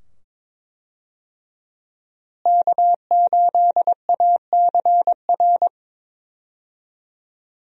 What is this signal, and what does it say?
Morse code, heard as a keyed tone, about 21 wpm, 700 Hz. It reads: K8ACR
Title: USA Call Signs - 5 Characters - Repeated 2x - 22wpm